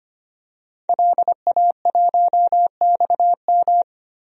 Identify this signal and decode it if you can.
Morse code, heard as a keyed tone, about 25 wpm, 700 Hz. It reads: LA1XM
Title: DXCC-100 Call Signs - 5 Characters - 25wpm